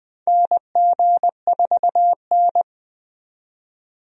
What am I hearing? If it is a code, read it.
Morse code, heard as a keyed tone, about 20 wpm, 700 Hz. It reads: NG4N
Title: USA Call Signs - 4 Characters - 20wpm